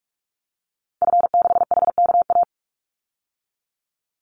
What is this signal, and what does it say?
Morse code, heard as a keyed tone, about 44 wpm, 700 Hz. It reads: F6HKA
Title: Call Signs Repeated 2x - 45wpm